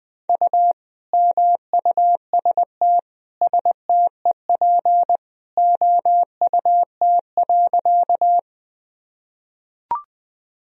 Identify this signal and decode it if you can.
Morse code, heard as a keyed tone, about 21 wpm, 700 Hz. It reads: U MUST STEP OUT.
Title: Sentences from Top 400 Words and Abbreviations 20wpm